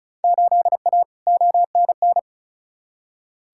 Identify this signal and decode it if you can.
Morse code, heard as a keyed tone, about 36 wpm, 700 Hz. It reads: 8A ONN
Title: Field Day Exchanges Encore - 35wpm